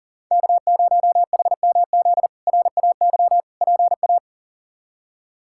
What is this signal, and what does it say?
Morse code, heard as a keyed tone, about 37 wpm, 700 Hz. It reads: K0HMZ RAY PA
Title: K1USN SST QSO Practice - 40wpm